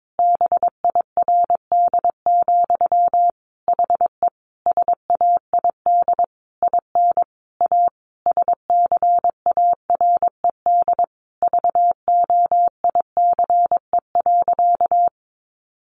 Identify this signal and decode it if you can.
Morse code, heard as a keyed tone, about 22 wpm, 700 Hz. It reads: BIRD, HE SAID IN A SCARED VOICE.
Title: Sentences from Top 700 Words 22wpm